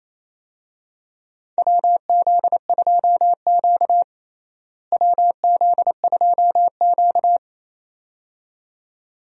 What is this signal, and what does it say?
Morse code, heard as a keyed tone, about 28 wpm, 700 Hz. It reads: WZ2Q WZ2Q
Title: USA Call Signs - 4 Characters - Repeated 2x - 28wpm